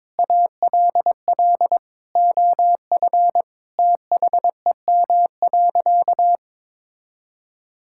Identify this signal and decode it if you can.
Morse code, heard as a keyed tone, about 22 wpm, 700 Hz. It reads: ALL OF THEM.
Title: Sentences from Top 1000 Words - Review Part 5 - 22wpm